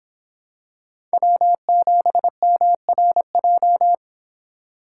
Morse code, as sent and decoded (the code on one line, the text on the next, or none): .-- --... -- .-. .---
W7MRJ